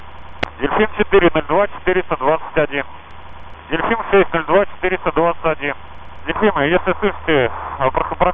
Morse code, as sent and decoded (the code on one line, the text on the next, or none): none